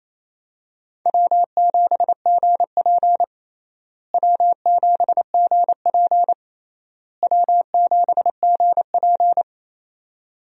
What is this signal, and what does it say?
Morse code, heard as a keyed tone, about 27 wpm, 700 Hz. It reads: W7GP W7GP W7GP
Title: USA Call Signs - 4 Characters - Repeated 3x - 28wpm